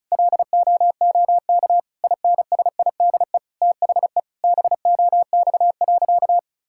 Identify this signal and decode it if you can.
Morse code, heard as a keyed tone, about 36 wpm, 700 Hz. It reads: LOOK INSIDE THE BOX.
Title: Sentences from Top 1000 Words - Review Part 6 - 35wpm